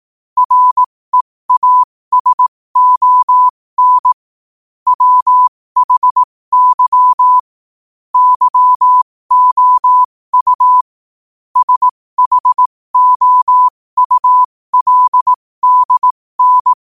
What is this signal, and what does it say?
Morse code, heard as a keyed tone, about 18 wpm, spaced as for 15 wpm, 1000 Hz. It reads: REASON WHY YOU SHOULDN